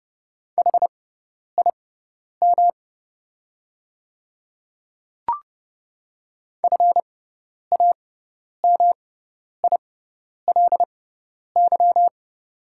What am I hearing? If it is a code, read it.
Morse code, heard as a keyed tone, about 30 wpm, spaced as for 10 wpm, 690 Hz. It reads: HIM FAMILY
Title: 6x Character Spacing - Top 500 Words - 30wpm